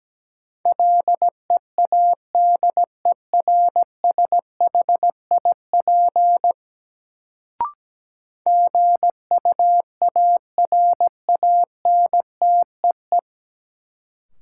Morse code, heard as a keyed tone, about 17 wpm, 700 Hz. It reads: LEADERSHIP GUARANTEE
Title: Sentences from Top 1600 Words - New Word Drill Alt - 17wpm